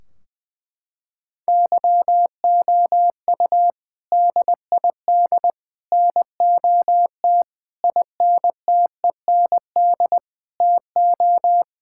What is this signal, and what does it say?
Morse code, heard as a keyed tone, about 21 wpm, 700 Hz. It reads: YOU DID NOT INTEND TO
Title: Sentences from Top 1600 Words 20wpm